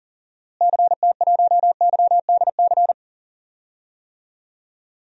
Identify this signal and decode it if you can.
Morse code, heard as a keyed tone, about 37 wpm, 700 Hz. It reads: CT1YDC
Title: DXCC-100 Call Signs - 6 Characters - Repeated 3x - 40wpm